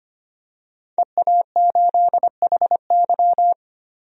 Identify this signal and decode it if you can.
Morse code, heard as a keyed tone, about 25 wpm, 700 Hz. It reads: EA8HY
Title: DXCC-100 Call Signs - 5 Characters - 25wpm